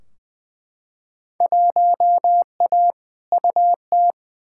Morse code, heard as a keyed tone, about 20 wpm, 700 Hz. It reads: 1A UT
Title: Field Day Exchanges - 20wpm